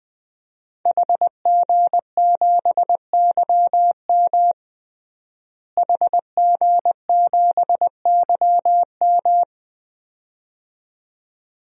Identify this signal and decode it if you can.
Morse code, heard as a keyed tone, about 20 wpm, 700 Hz. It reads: HG7YM HG7YM
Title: DXCC-100 Call Signs - 5 Characters - Repeated 2x - 20wpm